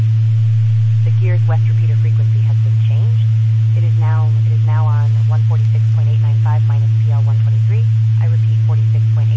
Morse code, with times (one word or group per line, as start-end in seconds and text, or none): none